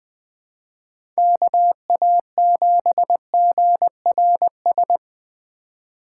Morse code, heard as a keyed tone, about 20 wpm, 700 Hz. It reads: KA7GRS
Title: USA Call Signs - 2x Character Spacing - 20wpm